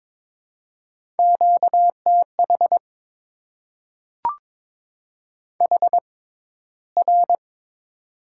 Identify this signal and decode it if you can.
Morse code, heard as a keyed tone, about 22 wpm, spaced as for 14 wpm, 700 Hz. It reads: QTH HR
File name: Top 90 QSO Elements - 6x Character Spacing - 22wpm